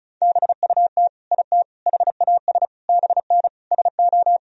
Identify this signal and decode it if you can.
Morse code, heard as a keyed tone, about 36 wpm, 700 Hz. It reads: BUT IT HAS BN SO
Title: Sentences from Top 100 Words and Abbreviations 35wpm